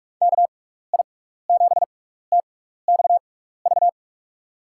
Morse code, heard as a keyed tone, about 47 wpm, spaced as for 16 wpm, 700 Hz. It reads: KI7TXV
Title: USA Call Signs - 6 Characters - 6x Character Spacing - 45wpm